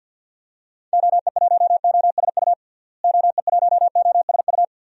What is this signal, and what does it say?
Morse code, heard as a keyed tone, about 50 wpm, 700 Hz. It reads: OE1OSU OE1OSU
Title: DXCC-100 Call Signs - 6 Characters - Repeated 2x - 50wpm